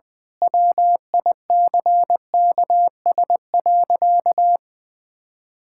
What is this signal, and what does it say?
Morse code, heard as a keyed tone, about 20 wpm, 700 Hz. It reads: WICKS.
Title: Pangram Challenge v2 - 20wpm